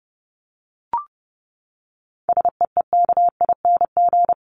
Morse code, heard as a keyed tone, about 30 wpm, 700 Hz. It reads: SEEKING
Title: ING Suffix - Top 100 Words with Base Words 30wpm